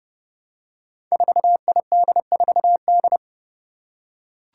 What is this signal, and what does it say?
Morse code, heard as a keyed tone, about 28 wpm, 700 Hz. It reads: 4ID4D
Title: Five Challenging Letter-Number Combinations 30wpm